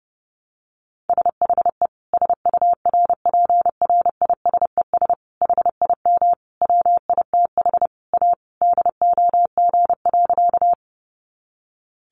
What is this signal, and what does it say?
Morse code, heard as a keyed tone, about 30 wpm, 700 Hz. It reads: SHE SURPRISES HIM WITH A DOG.
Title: Additional Easy Phrases 30wpm